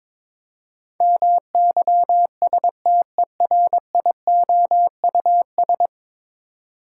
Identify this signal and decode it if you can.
Morse code, heard as a keyed tone, about 22 wpm, 700 Hz. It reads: MYSTERIOUS
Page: morse-code-ninja-practice-22wpm